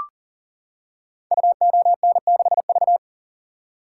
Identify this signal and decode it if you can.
Morse code, heard as a keyed tone, about 42 wpm, 700 Hz. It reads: UON6V
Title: Code Groups - 5 Letter:Numbers - 40wpm